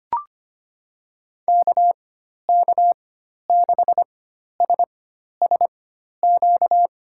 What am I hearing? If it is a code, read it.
Morse code, heard as a keyed tone, about 25 wpm, spaced as for 12 wpm, 700 Hz. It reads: KK6SSQ
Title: USA Call Signs - 4x Character Spacing - 25wpm